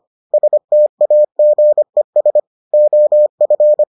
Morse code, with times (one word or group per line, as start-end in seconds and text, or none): 0.3-2.4 STAGES
2.7-3.8 OF